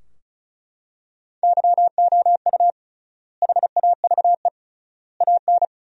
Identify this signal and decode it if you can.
Morse code, heard as a keyed tone, about 36 wpm, 700 Hz. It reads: YOU HAVE AN
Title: 3x Word-Spacing - Sentences from Top 400 Words - 35wpm